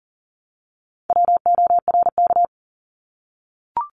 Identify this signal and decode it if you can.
Morse code, heard as a keyed tone, about 39 wpm, 700 Hz. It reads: WORK
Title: Multi-letter Phonemes 40wpm